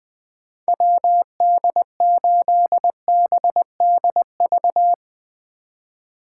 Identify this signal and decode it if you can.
Morse code, heard as a keyed tone, about 22 wpm, 700 Hz. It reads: WD8BDV
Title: Speed Racing - USA 2x3 Call Signs 20wpm